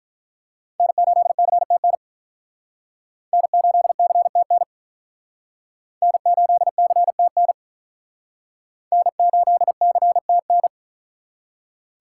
Morse code, heard as a keyed tone, about 48 wpm, 700 Hz. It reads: N8CTN N8CTN N8CTN N8CTN